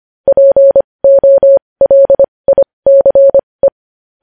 Morse code, heard as a keyed tone, about 25 wpm, spaced as for 20 wpm, 550 Hz. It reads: POLICE
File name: podcast / News headlines-1x-25x20